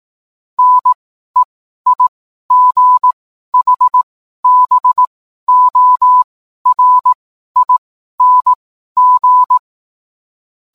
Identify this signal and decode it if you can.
Morse code, heard as a keyed tone, about 18 wpm, spaced as for 12 wpm, 1000 Hz. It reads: NEIGHBORING